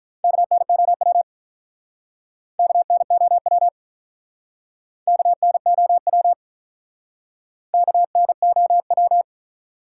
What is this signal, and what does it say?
Morse code, heard as a keyed tone, about 46 wpm, 700 Hz. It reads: KNOW KNOW KNOW KNOW